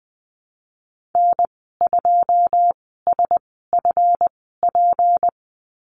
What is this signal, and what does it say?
Morse code, heard as a keyed tone, about 20 wpm, spaced as for 15 wpm, 700 Hz. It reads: N2SFP
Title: 186 Lesson 30g - Letter b - Call Signs 20wpm